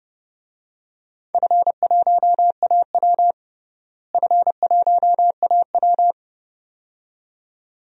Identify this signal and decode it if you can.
Morse code, heard as a keyed tone, about 30 wpm, 700 Hz. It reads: F1AW F1AW
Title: DXCC-100 Call Signs - 4 Characters - Repeated 2x - 30wpm